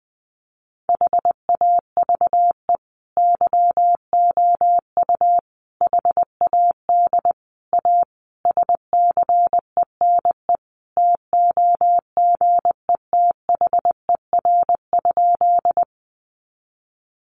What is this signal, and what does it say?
Morse code, heard as a keyed tone, about 20 wpm, 700 Hz. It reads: HAVE YOU HAD A SCENE TOGETHER?
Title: Sentences from Top 1100 Words 20wpm